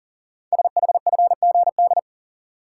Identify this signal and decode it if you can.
Morse code, heard as a keyed tone, about 41 wpm, 700 Hz. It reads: SHFGD
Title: Code Groups - 5 Letters - 40wpm